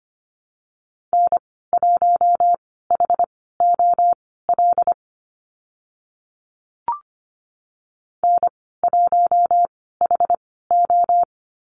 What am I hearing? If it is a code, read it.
Morse code, heard as a keyed tone, about 25 wpm, spaced as for 16 wpm, 700 Hz. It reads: N1HOL N1HO
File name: Lesson 15g - Number 5 - Call Signs 25wpm